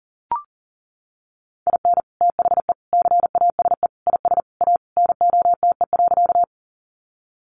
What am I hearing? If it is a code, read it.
Morse code, heard as a keyed tone, about 40 wpm, 700 Hz. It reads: IN THE CASE IS A NOTE.
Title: Sentences from Top 700 Words 40wpm